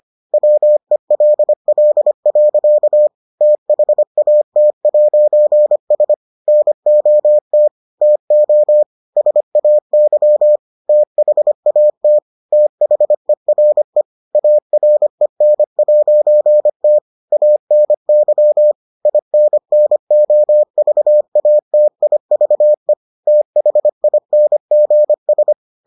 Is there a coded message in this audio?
Yes: Morse code from 0.3 to 25.5 s, heard as a keyed tone, about 25 wpm, 600 Hz. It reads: WELL. THAT'S NOT TO SAY THAT THERE AREN'T ANY INNOVATIVE THINGS